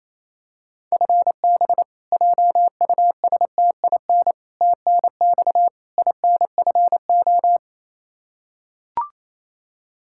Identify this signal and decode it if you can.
Morse code, heard as a keyed tone, about 28 wpm, 700 Hz. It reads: FB JUSTIN TNX INFO
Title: Practice QSO - Exchange 3 - More Info - 28wpm